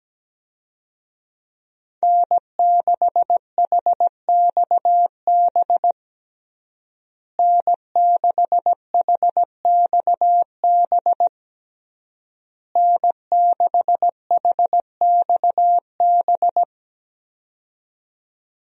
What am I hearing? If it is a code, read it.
Morse code, heard as a keyed tone, about 17 wpm, 700 Hz. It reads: N6HXB N6HXB N6HXB
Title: USA Call Signs - 5 Characters - Repeated 3x - 17wpm